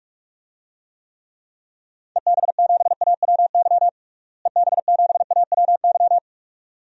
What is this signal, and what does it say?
Morse code, heard as a keyed tone, about 46 wpm, 700 Hz. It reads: EB7AWY EB7AWY
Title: DXCC-100 Call Signs - 6 Characters - Repeated 2x - 45wpm